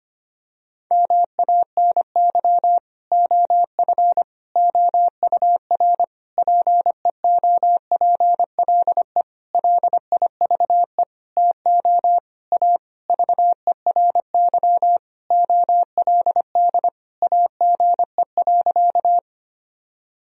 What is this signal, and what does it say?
Morse code, heard as a keyed tone, about 25 wpm, 700 Hz. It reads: MANY OF OUR PEOPLE LIVE TO A VERY OLD AGE.
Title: Sentences from Top 1000 Words - Review Part 3 - 25wpm